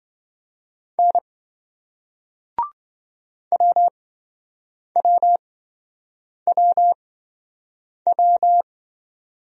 Morse code, heard as a keyed tone, about 28 wpm, spaced as for 7 wpm, 700 Hz. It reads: N WWWW